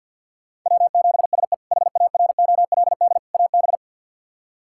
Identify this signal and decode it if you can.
Morse code, heard as a keyed tone, about 47 wpm, 690 Hz. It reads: W7SE HAROLD AB
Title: K1USN SST QSO Practice - 50wpm